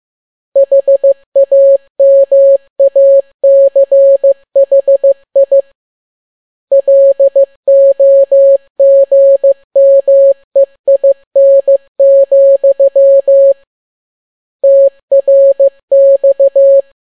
Morse code, heard as a keyed tone, about 15 wpm, 550 Hz. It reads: HAMACHI LOGMEIN, TRX